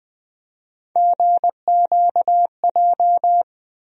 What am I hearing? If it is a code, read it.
Morse code, heard as a keyed tone, about 20 wpm, 700 Hz. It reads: GQJ
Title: USA Call Sign Postfixes - 3 Letters 20wpm